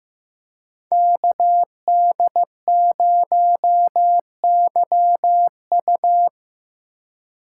Morse code, heard as a keyed tone, about 17 wpm, 700 Hz. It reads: KD0YU
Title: Speed Racing - USA 2x2 Call Signs 15wpm